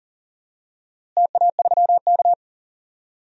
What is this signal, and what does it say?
Morse code, heard as a keyed tone, about 40 wpm, 700 Hz. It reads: TA3K
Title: DXCC-100 Call Signs - 4 Characters - Repeated 3x - 40wpm